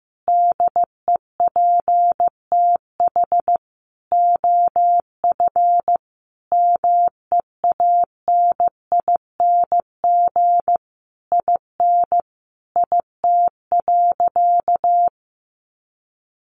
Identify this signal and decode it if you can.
Morse code, heard as a keyed tone, about 15 wpm, 700 Hz. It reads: DEPTH OF MEANING IN IT.
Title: Sentences from Top 1200 Words 15wpm